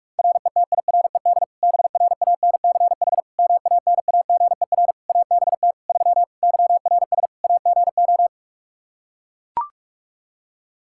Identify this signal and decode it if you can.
Morse code, heard as a keyed tone, about 42 wpm, 700 Hz. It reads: RETIRED BRANCH MANAGER ABT 3 YRS AGO
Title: Practice QSO - Exchange 3 - More Info - 45wpm